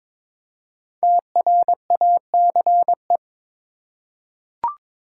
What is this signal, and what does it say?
Morse code, heard as a keyed tone, about 22 wpm, 700 Hz. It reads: TRACE
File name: 5 Letter Words - First 3 Letters Given - 22wpm